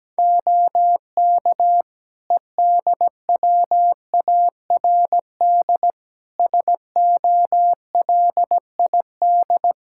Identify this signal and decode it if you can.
Morse code, heard as a keyed tone, about 17 wpm, 700 Hz. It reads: OK EDWARD SOLID
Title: Practice QSO - Exchange 3 - More Info - 17wpm